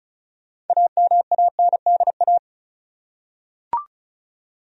Morse code, heard as a keyed tone, about 34 wpm, 700 Hz. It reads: AMANDA
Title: USA Common Names - 35wpm